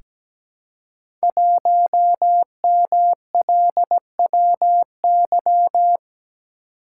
Morse code, heard as a keyed tone, about 17 wpm, 700 Hz. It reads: 1MLWY
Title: Code Groups - 5 Letter:Numbers - 17wpm